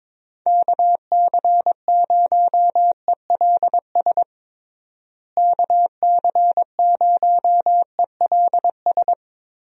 Morse code, heard as a keyed tone, about 22 wpm, 700 Hz. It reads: KC0ELS KC0ELS
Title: USA Call Signs - 6 Characters - Repeated 3x - 22wpm